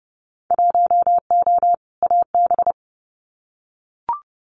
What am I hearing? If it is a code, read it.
Morse code, heard as a keyed tone, about 30 wpm, 700 Hz. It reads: 1O AB
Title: Winter Field Day Exchanges Encore 30wpm